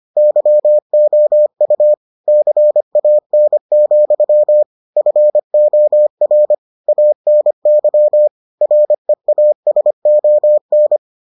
Morse code, heard as a keyed tone, about 25 wpm, 600 Hz. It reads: YOU CAN, FOR ANY REASON